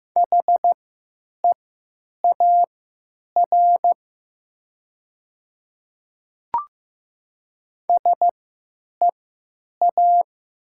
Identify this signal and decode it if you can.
Morse code, heard as a keyed tone, about 15 wpm, spaced as for 9 wpm, 700 Hz. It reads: HEAR SEA
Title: 3x Character Spacing - Top 300 Words - 15wpm